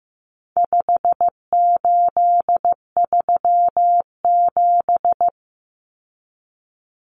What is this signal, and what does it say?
Morse code, heard as a keyed tone, about 15 wpm, 700 Hz. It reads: 5837
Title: Numbers - 4-Digit 15wpm